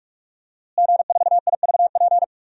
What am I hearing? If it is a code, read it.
Morse code, heard as a keyed tone, about 41 wpm, 700 Hz. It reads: G4IVP
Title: DXCC-100 Call Signs - 5 Characters - Repeated 3x - 45wpm